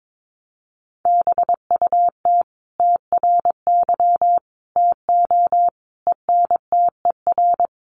Morse code, heard as a keyed tone, about 22 wpm, 700 Hz. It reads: BUT TRY TO ENTER
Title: Sentences from Top 1300 Words 22wpm